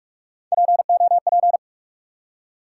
Morse code, heard as a keyed tone, about 44 wpm, 700 Hz. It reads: POP
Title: Three Letter Words - 45wpm